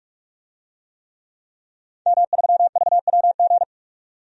Morse code, heard as a keyed tone, about 44 wpm, 700 Hz. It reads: M3VWG
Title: DXCC-100 Call Signs - 5 Characters - 45wpm